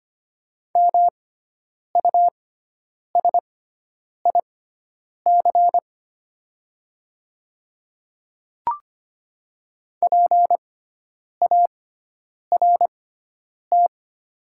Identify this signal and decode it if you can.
Morse code, heard as a keyed tone, about 25 wpm, spaced as for 9 wpm, 690 Hz. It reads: MUSIC PART
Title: 6x Character Spacing - Top 500 Words - 25wpm